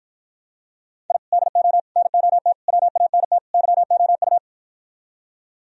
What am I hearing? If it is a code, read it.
Morse code, heard as a keyed tone, about 52 wpm, 700 Hz. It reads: I DO NOT WANT YOU